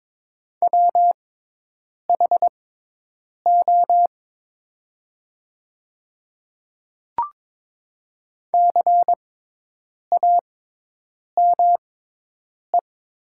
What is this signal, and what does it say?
Morse code, heard as a keyed tone, about 22 wpm, spaced as for 8 wpm, 700 Hz. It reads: WHO CAME
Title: 6x Character Spacing - Top 300 Words - 22wpm